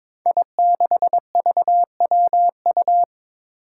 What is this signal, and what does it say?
Morse code, heard as a keyed tone, about 22 wpm, 700 Hz. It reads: I6VWU
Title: Five Challenging Letter-Number Combinations 22wpm